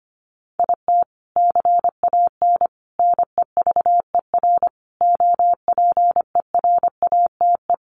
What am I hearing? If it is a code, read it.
Morse code, heard as a keyed tone, about 25 wpm, 700 Hz. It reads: IT CAN NEVER OPERATE
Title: Sentences from Top 1500 Words 25wpm